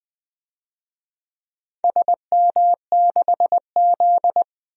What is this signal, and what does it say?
Morse code, heard as a keyed tone, about 20 wpm, 700 Hz. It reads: SM6Z
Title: DXCC-100 Call Signs - 4 Characters - Repeated 2x - 20wpm